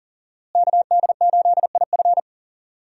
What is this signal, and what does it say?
Morse code, heard as a keyed tone, about 40 wpm, 700 Hz. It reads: KD8IF
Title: USA Call Signs - 5 Characters - Repeated 2x - 40wpm